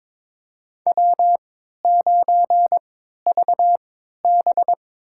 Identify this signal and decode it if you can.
Morse code, heard as a keyed tone, about 22 wpm, spaced as for 13 wpm, 700 Hz. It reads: W9VB
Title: USA Call Signs - 4 Characters - 3x Character Spacing - 22wpm